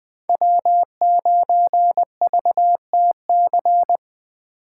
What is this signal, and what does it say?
Morse code, heard as a keyed tone, about 20 wpm, 700 Hz. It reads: W9VTC